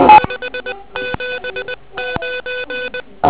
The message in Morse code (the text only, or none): 579